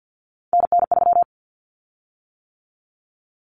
Morse code, heard as a keyed tone, about 50 wpm, 700 Hz. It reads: NN3
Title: Common Cut Numbers - 3-Digit 50wpm